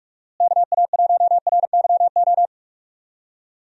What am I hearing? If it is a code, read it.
Morse code, heard as a keyed tone, about 46 wpm, 700 Hz. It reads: KA1RYO
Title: USA Call Signs - 6 Characters - Repeated 3x - 45wpm